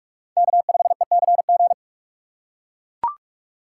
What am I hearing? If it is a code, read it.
Morse code, heard as a keyed tone, about 42 wpm, 700 Hz. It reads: K5ECG
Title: USA Call Signs - 5 Characters - 2x Character Spacing - 45wpm